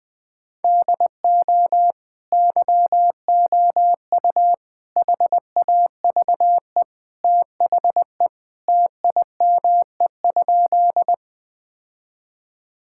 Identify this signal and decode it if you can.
Morse code, heard as a keyed tone, about 19 wpm, 700 Hz. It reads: DO YOU HAVE THE TIME?
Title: Sentences with 2-Word Phrases - Spoken Beginning - 20wpm